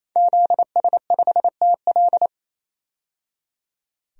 Morse code, heard as a keyed tone, about 27 wpm, 700 Hz. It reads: ZS5TL
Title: DXCC-100 Call Signs - 5 Characters - 28wpm